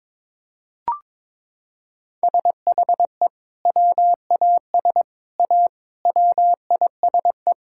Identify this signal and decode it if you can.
Morse code, heard as a keyed tone, about 22 wpm, 700 Hz. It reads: SHE WAS A WISE
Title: Sentences from Top 1600 Words 22wpm